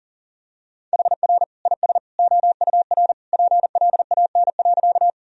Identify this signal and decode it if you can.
Morse code, heard as a keyed tone, about 40 wpm, 700 Hz. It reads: HR IS OUR PLAN.
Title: Sentences from Top 500 Words and Abbreviations 40wpm